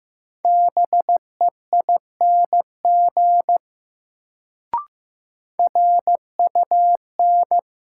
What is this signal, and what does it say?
Morse code, heard as a keyed tone, about 15 wpm, 700 Hz. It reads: BEING RUN
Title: Sets of 4 Words - New Word Drill - 15wpm